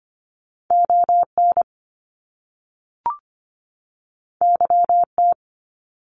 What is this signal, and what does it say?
Morse code, heard as a keyed tone, about 25 wpm, 700 Hz. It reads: ON YT
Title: Canadian Provinces and Territories 25wpm